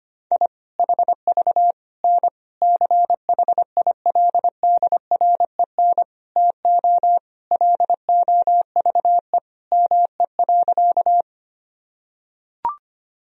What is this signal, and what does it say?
Morse code, heard as a keyed tone, about 25 wpm, 700 Hz. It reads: I HV N CHILDREN TO LOVE ME.
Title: Sentences from Top 400 Words and Abbreviations 25wpm